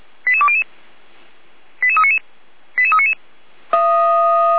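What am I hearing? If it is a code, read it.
Five-tone selective call (ZVEI), heard as quick runs of five tones, tones of about 70 ms. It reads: 88200 88399 88399
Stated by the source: ZVEI five-tone selective call, code 88314, received by SDR.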